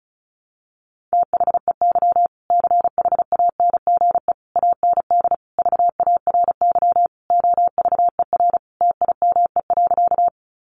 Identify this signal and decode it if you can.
Morse code, heard as a keyed tone, about 34 wpm, 700 Hz. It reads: THEY CHANGE AND VARY OVER TIME.
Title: Sentences from Top 1200 Words 35wpm